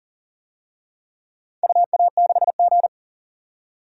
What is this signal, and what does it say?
Morse code, heard as a keyed tone, about 40 wpm, 700 Hz. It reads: UA6G
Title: DXCC-100 Call Signs - 4 Characters - 40wpm